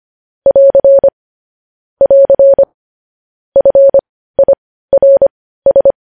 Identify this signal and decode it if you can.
Morse code, heard as a keyed tone, about 25 wpm, spaced as for 15 wpm, 550 Hz. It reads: <AR> <AR> FIRS
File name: podcast / News headlines-1x-25x15